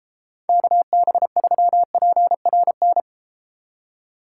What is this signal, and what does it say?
Morse code, heard as a keyed tone, about 32 wpm, 700 Hz. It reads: KB3PRN